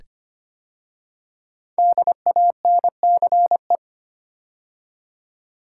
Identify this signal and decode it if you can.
Morse code, heard as a keyed tone, about 25 wpm, 700 Hz. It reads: DANCE
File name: 5 Letter Words - First 3 Letters Given - 25wpm